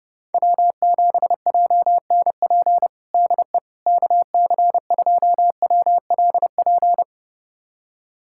Morse code, heard as a keyed tone, about 30 wpm, 700 Hz. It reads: W7JNP DE KC2WLP
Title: USA Callsign DE Callsign 30wpm